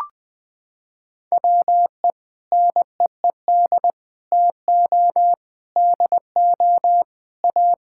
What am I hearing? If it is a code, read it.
Morse code, heard as a keyed tone, about 20 wpm, 700 Hz. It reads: WE NEED TO DO A
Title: Sentences with 6-Word Phrases - 20wpm